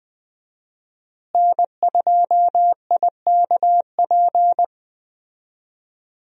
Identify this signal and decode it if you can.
Morse code, heard as a keyed tone, about 20 wpm, 700 Hz. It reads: N2IKP
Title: USA Call Signs - 5 Characters - Repeated 3x - 20wpm